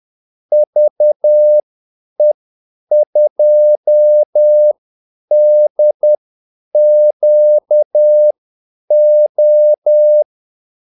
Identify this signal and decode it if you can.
Morse code, heard as a keyed tone, about 10 wpm, spaced as for 8 wpm, 600 Hz. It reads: VE2DQO
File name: Callsigns-10-8wpm